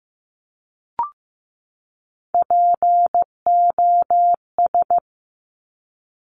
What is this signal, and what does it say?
Morse code, heard as a keyed tone, about 15 wpm, 700 Hz. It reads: POS
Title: Top Three Letter Combinations in Words 15wpm